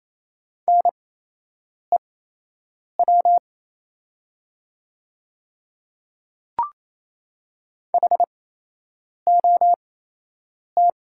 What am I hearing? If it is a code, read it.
Morse code, heard as a keyed tone, about 28 wpm, spaced as for 8 wpm, 700 Hz. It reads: NEW HOT
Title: 8x Character Spacing - Top 300 Words - 28wpm